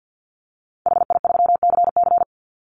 Morse code, heard as a keyed tone, about 50 wpm, 700 Hz. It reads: HI3CC
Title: Call Signs - 50wpm